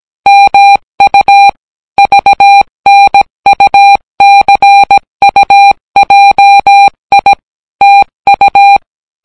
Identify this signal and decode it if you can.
Morse code, heard as a keyed tone, about 17 wpm, 800 Hz. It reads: MU VNUCUJI TU